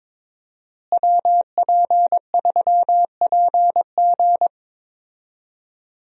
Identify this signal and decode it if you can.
Morse code, heard as a keyed tone, about 22 wpm, 700 Hz. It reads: WP3PG
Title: USA Call Signs - 5 Characters - Repeated 3x - 22wpm